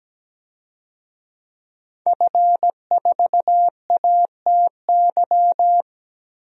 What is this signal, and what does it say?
Morse code, heard as a keyed tone, about 17 wpm, 700 Hz. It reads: F4ATY
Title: DXCC-100 Call Signs - 5 Characters - 17wpm